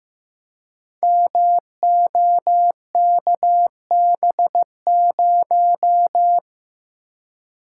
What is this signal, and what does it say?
Morse code, heard as a keyed tone, about 15 wpm, 700 Hz. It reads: MOKB0
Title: Code Groups - 5 Letter:Numbers - 15wpm